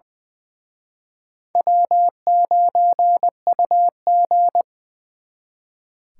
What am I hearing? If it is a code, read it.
Morse code, heard as a keyed tone, about 24 wpm, 700 Hz. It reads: W9UG
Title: Speed Racing - USA Call Signs 20wpm